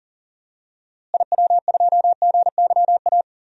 Morse code, heard as a keyed tone, about 40 wpm, 700 Hz. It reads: IW2GYA
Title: DXCC-100 Call Signs - 6 Characters - Repeated 2x - 40wpm